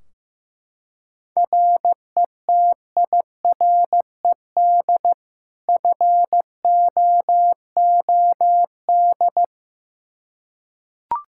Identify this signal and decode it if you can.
Morse code, heard as a keyed tone, about 15 wpm, 700 Hz. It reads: RETIRED FOOD
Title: K1USN SST QSO Practice 15wpm